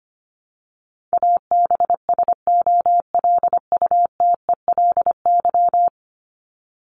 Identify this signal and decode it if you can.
Morse code, heard as a keyed tone, about 25 wpm, 700 Hz. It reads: ABSOLUTELY
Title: LY Suffix - Top 100 Words with Base Words 25wpm